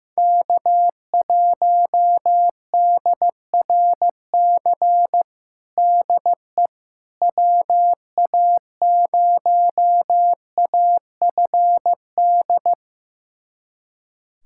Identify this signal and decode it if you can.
Morse code, heard as a keyed tone, about 15 wpm, 700 Hz. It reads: K1DRC DE WA0AFD
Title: USA Callsign DE Callsign 15wpm